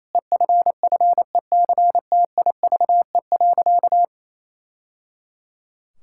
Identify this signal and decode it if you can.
Morse code, heard as a keyed tone, about 28 wpm, 700 Hz. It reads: EFFECTIVE.
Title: Sentences from Top 1000 Words - Review Part 6 - 28wpm